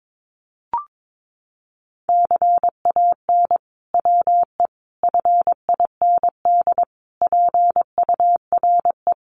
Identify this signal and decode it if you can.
Morse code, heard as a keyed tone, about 22 wpm, 700 Hz. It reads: CAN WE FIND PURE
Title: Sentences from Top 1500 Words 22wpm